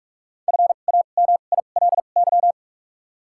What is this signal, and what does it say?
Morse code, heard as a keyed tone, about 46 wpm, spaced as for 33 wpm, 700 Hz. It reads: FAMILY
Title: 2x Character Spacing - Top 500 Words - 45wpm